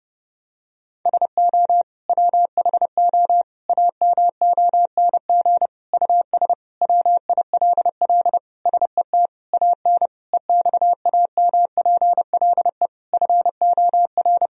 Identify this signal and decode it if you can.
Morse code, heard as a keyed tone, about 30 wpm, 700 Hz. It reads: SO WHO AMONG US WILL SET AN EXAMPLE FOR
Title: Sentences from Top 1000 Words - Review Part 3 - 30wpm